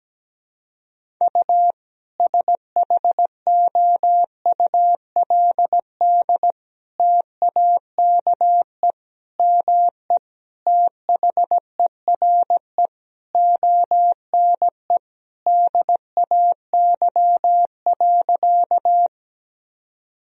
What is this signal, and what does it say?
Morse code, heard as a keyed tone, about 17 wpm, 700 Hz. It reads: U SHOULD TAKE ME THERE ONE DAY.
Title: Sentences from Top 300 Words and Abbreviations 17wpm